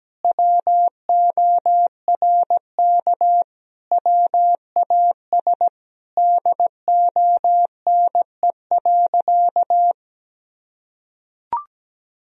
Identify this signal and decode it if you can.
Morse code, heard as a keyed tone, about 17 wpm, 700 Hz. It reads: WORK WAS DONE.
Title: Sentences from Top 1000 Words - Review Part 1 - 17wpm